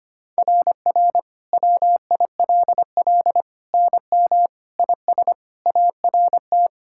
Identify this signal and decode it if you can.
Morse code, heard as a keyed tone, about 25 wpm, 700 Hz. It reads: RR WILL NM IS ART
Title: SOTA QSO Practice 25wpm